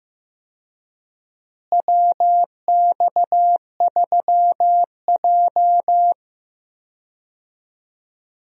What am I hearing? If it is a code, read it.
Morse code, heard as a keyed tone, about 15 wpm, 700 Hz. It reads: WX3J
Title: USA Call Signs - 4 Characters - Repeated 2x - 15wpm